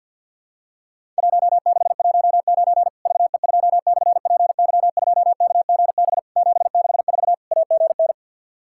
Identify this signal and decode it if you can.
Morse code, heard as a keyed tone, about 50 wpm, 700 Hz. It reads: 1619 VE2/PY2KGB 764 AGN
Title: Worked All Europe DX Contest - QTC Exchange with Repeats - 50wpm